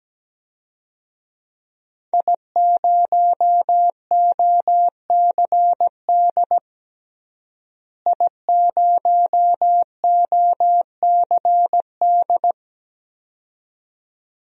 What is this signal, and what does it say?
Morse code, heard as a keyed tone, about 17 wpm, 700 Hz. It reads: I0OCD I0OCD
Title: DXCC-100 Call Signs - 5 Characters - Repeated 2x - 17wpm